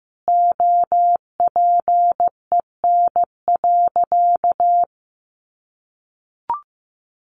Morse code, heard as a keyed tone, about 15 wpm, 700 Hz. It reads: OPEN.
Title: Sentences from Top 1000 Words 15wpm